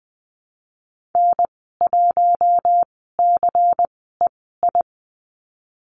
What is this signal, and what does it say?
Morse code, heard as a keyed tone, about 20 wpm, spaced as for 15 wpm, 700 Hz. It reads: N1CEI
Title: 140 Lesson 23g - Letter f - Call Signs 20wpm